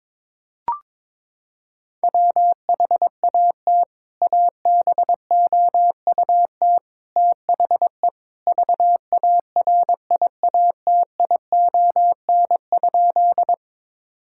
Morse code, heard as a keyed tone, about 21 wpm, 700 Hz. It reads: WHAT ABOUT THE VARIATION?
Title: Sentences from Top 1600 Words 22wpm